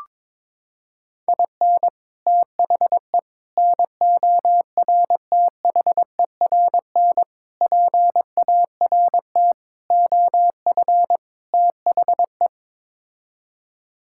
Morse code, heard as a keyed tone, about 22 wpm, 700 Hz. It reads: IN THE NORTHERN PART OF THE
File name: practice / 22wpm / Sets of 6 Words - 22wpm